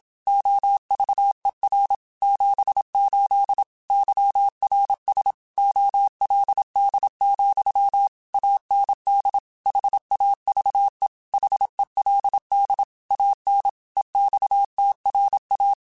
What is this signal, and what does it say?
Morse code, heard as a keyed tone, about 27 wpm, 800 Hz. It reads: OVER 78 YRS OLD, AND HAVE HELD AN EXTRA
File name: QSOs / QSOs-mp3s / kc7sac@25